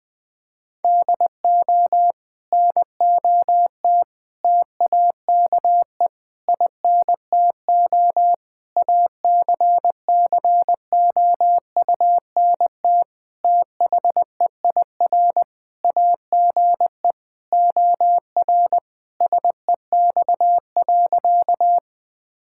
Morse code, heard as a keyed tone, about 20 wpm, 700 Hz. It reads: DO NOT TAKE INTO ACCOUNT THEIR AGE OR SEX.
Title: Sentences from Top 1600 Words 20wpm